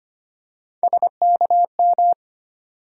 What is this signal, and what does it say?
Morse code, heard as a keyed tone, about 25 wpm, 700 Hz. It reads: SKM
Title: USA Call Sign Postfixes - 3 Letters 25wpm